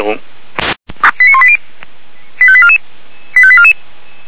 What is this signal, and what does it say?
Five-tone selective call (ZVEI), heard as quick runs of five tones, tones of about 70 ms, with speech. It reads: 89180 85844 85844